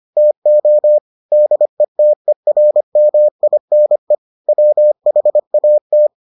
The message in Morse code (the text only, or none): TO DETERMINE WHAT